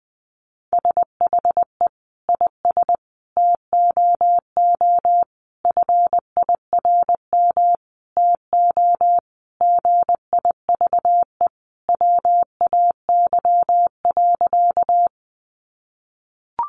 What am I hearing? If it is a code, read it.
Morse code, heard as a keyed tone, about 20 wpm, 700 Hz. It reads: SHE IS TOO FIRM TO GIVE WAY.
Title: Sentences from Top 1400 Words 20wpm